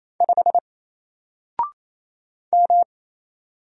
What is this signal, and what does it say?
Morse code, heard as a keyed tone, about 27 wpm, 700 Hz. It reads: 5 M
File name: Single-Letter-Number-28wpm